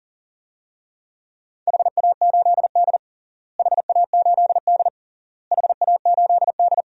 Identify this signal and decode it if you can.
Morse code, heard as a keyed tone, about 37 wpm, 700 Hz. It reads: HA8D HA8D HA8D
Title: DXCC-100 Call Signs - 4 Characters - Repeated 3x - 40wpm